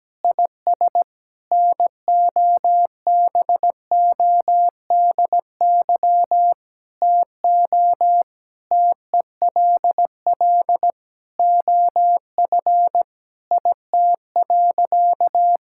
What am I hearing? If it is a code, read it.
Morse code, heard as a keyed tone, about 17 wpm, 700 Hz. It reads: IS NOBODY TO TELL OF IT.
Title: Sentences from Top 1600 Words 17wpm